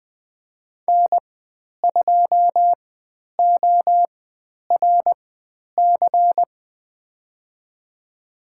Morse code, heard as a keyed tone, about 20 wpm, spaced as for 10 wpm, 700 Hz. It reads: N2ORC
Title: 102 Lesson 17g - Letter u - Call Signs 20wpm